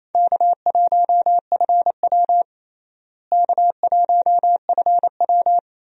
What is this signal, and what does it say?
Morse code, heard as a keyed tone, about 28 wpm, 700 Hz. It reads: K1FW K1FW
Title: USA Call Signs - 4 Characters - Repeated 3x - 28wpm